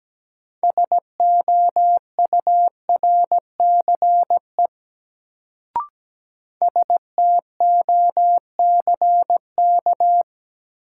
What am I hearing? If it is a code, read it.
Morse code, heard as a keyed tone, about 17 wpm, 700 Hz. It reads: SOURCE STOCK
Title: Sentences from Top 800 Words - New Word Drill - 17wpm